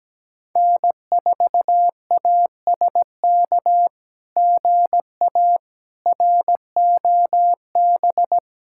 Morse code, heard as a keyed tone, about 17 wpm, 700 Hz. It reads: N4ASK GA ROB
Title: SOTA QSO Practice 17wpm